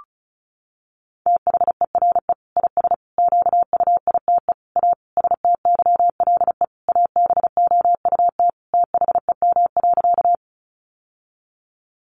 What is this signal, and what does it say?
Morse code, heard as a keyed tone, about 34 wpm, 700 Hz. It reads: THERE IS QUITE A STYLE ABOUT THEM.
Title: Sentences from Top 800 Words 35wpm